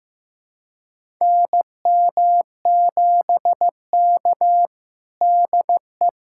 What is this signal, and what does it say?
Morse code, heard as a keyed tone, about 15 wpm, 700 Hz. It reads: NM7K DE
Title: USA Callsign DE Callsign - 4 Characters - 15wpm